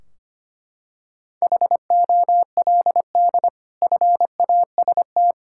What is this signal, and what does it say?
Morse code, heard as a keyed tone, about 25 wpm, 700 Hz. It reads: HOLD FAST
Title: Sentences from Top 1600 Words 25wpm